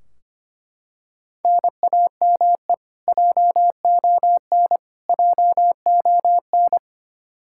Morse code, heard as a keyed tone, about 25 wpm, 700 Hz. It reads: NAME JON JON
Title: K1USN SST QSO Practice 25wpm